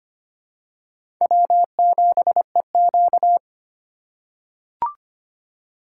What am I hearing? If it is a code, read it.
Morse code, heard as a keyed tone, about 25 wpm, 700 Hz. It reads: W7EQ
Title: USA Call Signs - 4 Characters - 8x Character Spacing - 25wpm